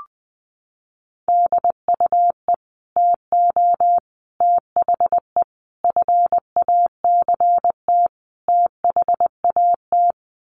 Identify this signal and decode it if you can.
Morse code, heard as a keyed tone, about 20 wpm, 700 Hz. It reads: DUE TO THE FACT THAT
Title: Sets of 5 Words Encore 20wpm